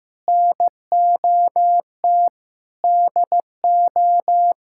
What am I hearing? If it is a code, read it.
Morse code, heard as a keyed tone, about 16 wpm, 700 Hz. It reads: NOT DO
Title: Sentences from Top 1000 Words - Review Part 6 - 15wpm